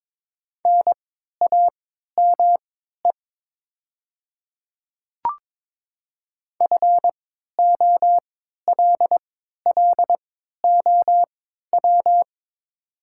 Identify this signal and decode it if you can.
Morse code, heard as a keyed tone, about 22 wpm, spaced as for 12 wpm, 700 Hz. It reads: NAME FOLLOW
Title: 3x Character Spacing - Top 200 Words - 22wpm